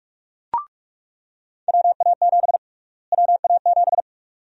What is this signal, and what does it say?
Morse code, heard as a keyed tone, about 47 wpm, 700 Hz. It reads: WA7 WA7
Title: USA Call Sign Prefixes - Repeated 2x - 45wpm